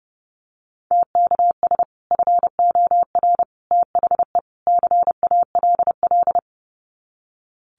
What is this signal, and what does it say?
Morse code, heard as a keyed tone, about 30 wpm, 700 Hz. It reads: TKS FOR THE CALL
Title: Common QSO Phrases 30wpm